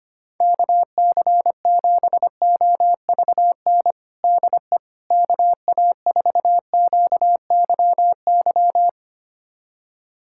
Morse code, heard as a keyed tone, about 25 wpm, 700 Hz. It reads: KC7OVN DE KA4QYY
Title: USA Callsign DE Callsign 25wpm